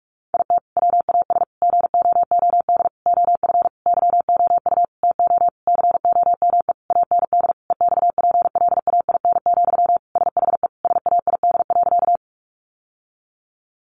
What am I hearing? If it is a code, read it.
Morse code, heard as a keyed tone, about 44 wpm, 700 Hz. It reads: IT WAS GOOD OF YOU TO COME AND EXPLAIN, SHE SAID.
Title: Sentences from Top 1100 Words 45wpm